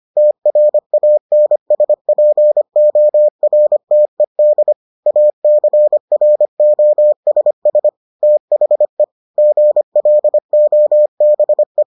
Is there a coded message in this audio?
Yes: Morse code from 0.2 to 11.8 s, heard as a keyed tone, about 25 wpm, 600 Hz. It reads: TRANSPORTED ACROSS THE GLOBE